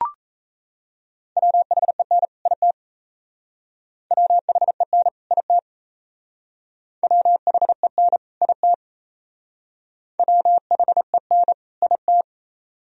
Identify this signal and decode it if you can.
Morse code, heard as a keyed tone, about 37 wpm, 700 Hz. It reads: WHEN IT WHEN IT WHEN IT WHEN IT